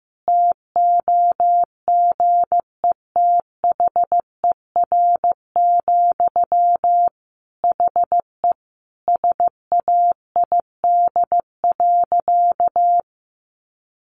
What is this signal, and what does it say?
Morse code, heard as a keyed tone, about 14 wpm, 700 Hz. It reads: TOGETHER, HE SAID.
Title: Sentences from Top 1100 Words 15wpm